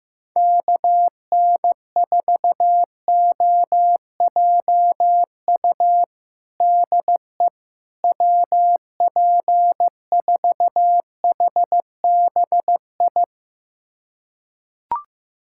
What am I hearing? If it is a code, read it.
Morse code, heard as a keyed tone, about 15 wpm, 700 Hz. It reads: KN4OJU DE WP4HBI
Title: USA Callsign DE Callsign 15wpm